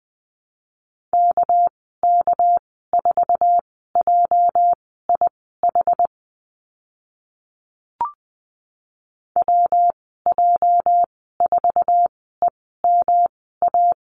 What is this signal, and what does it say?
Morse code, heard as a keyed tone, about 20 wpm, spaced as for 14 wpm, 700 Hz. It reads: KK4JIH WJ4EMA
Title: 207 Lesson 33g - Letter j - Call Signs 20wpm